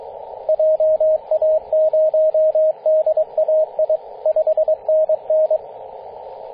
Morse code, heard as a keyed tone, about 22 wpm, 630 Hz, with radio noise. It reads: JA0DAI 5NN
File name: 9m0o-24-891mhz-cw